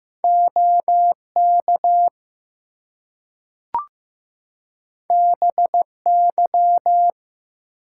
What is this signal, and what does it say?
Morse code, heard as a keyed tone, about 15 wpm, 700 Hz. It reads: OK BY
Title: Two Letter Words - 15wpm